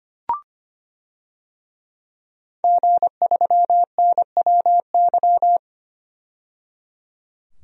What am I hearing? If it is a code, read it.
Morse code, heard as a keyed tone, about 25 wpm, 700 Hz. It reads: G3NWY
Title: DXCC-100 Call Signs - 5 Characters - 25wpm